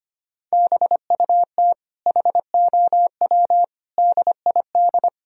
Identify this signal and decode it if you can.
Morse code, heard as a keyed tone, about 25 wpm, 700 Hz. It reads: BUT HOW DID
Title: Sentences from Top 1000 Words - Review Part 4 - 25wpm